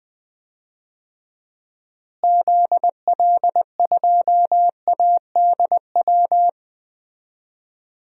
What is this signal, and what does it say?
Morse code, heard as a keyed tone, about 20 wpm, 700 Hz. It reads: ZL2ADW
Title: DXCC-100 Call Signs - 6 Characters - Repeated 3x - 20wpm